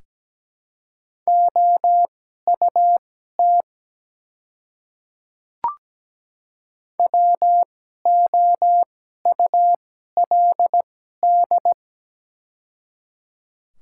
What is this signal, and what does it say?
Morse code, heard as a keyed tone, about 17 wpm, spaced as for 12 wpm, 690 Hz. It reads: OUT WOULD
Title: 2x Character Spacing - Top 100 Words - 17wpm